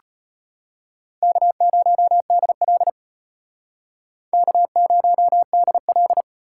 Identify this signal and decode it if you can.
Morse code, heard as a keyed tone, about 32 wpm, 700 Hz. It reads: K0DL K0DL